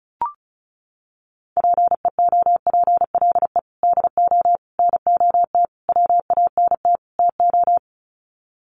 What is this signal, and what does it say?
Morse code, heard as a keyed tone, about 34 wpm, 700 Hz. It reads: PEOPLE DO NOT WANT TO
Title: Sets of 5 Words Encore2 35wpm